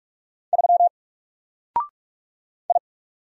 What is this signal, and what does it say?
Morse code, heard as a keyed tone, about 45 wpm, 700 Hz. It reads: 3 I
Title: Single Letter-Number Alt 45wpm